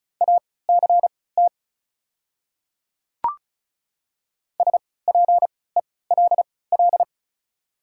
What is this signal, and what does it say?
Morse code, heard as a keyed tone, about 34 wpm, spaced as for 20 wpm, 700 Hz. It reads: ACT SPELL
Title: 3x Character Spacing - Top 500 Words - 35wpm